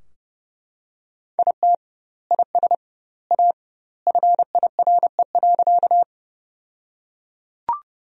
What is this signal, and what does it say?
Morse code, heard as a keyed tone, about 30 wpm, 700 Hz. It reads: IT IS A FIRE.
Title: 2x Word-Spacing - Sentences from Top 400 Words - 30wpm